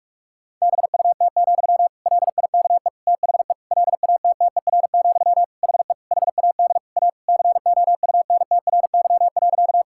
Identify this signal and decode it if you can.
Morse code, heard as a keyed tone, about 41 wpm, 700 Hz. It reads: BUT, LIKE THE LATTER, HE HAD A COUNTRY.
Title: Sentences from Top 1600 Words 45wpm